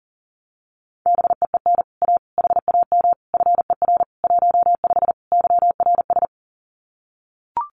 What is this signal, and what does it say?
Morse code, heard as a keyed tone, about 38 wpm, 700 Hz. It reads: BEEN A HAM FER 15 YRS
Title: Common QSO Phrases Encore 40wpm